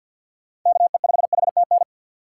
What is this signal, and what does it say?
Morse code, heard as a keyed tone, about 48 wpm, 700 Hz. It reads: KE5HTN
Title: USA Call Signs - 6x Character Spacing - 50wpm